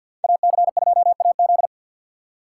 Morse code, heard as a keyed tone, about 61 wpm, 700 Hz. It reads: AC2AZ
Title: Speed Racing - USA 2x2 Call Signs 50wpm